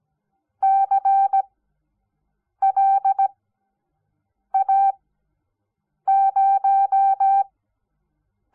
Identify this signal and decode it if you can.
Morse code, heard as a keyed tone, about 17 wpm, 780 Hz. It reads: C L A 0
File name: Onlinelehrgang / morsen / DJ4UF Morsen06